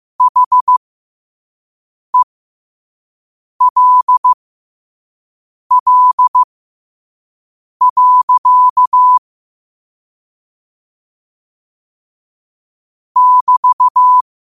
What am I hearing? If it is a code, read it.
Morse code, heard as a keyed tone, about 15 wpm, spaced as for 5 wpm, 1000 Hz. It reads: HELL. <BT>
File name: qotdfiles / 5wpm